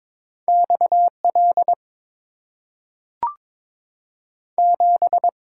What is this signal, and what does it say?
Morse code, heard as a keyed tone, about 22 wpm, 690 Hz. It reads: X L 7
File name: practice / 22wpm / Two Letter-Number Combinations - 22wpm